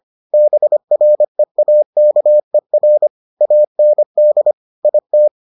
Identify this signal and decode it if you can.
Morse code, heard as a keyed tone, about 25 wpm, 600 Hz. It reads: BREAKER AND IT